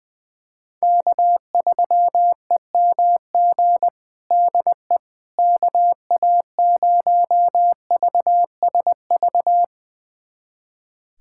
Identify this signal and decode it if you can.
Morse code, heard as a keyed tone, about 21 wpm, 700 Hz. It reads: K3EMG DE KA0VSV
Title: USA Callsign DE Callsign 20wpm